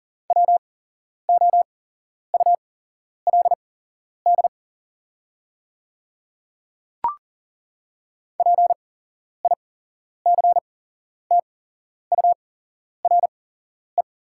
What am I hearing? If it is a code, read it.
Morse code, heard as a keyed tone, about 40 wpm, spaced as for 11 wpm, 700 Hz. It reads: WOULD PICTURE
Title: 8x Character Spacing - Top 300 Words - 40wpm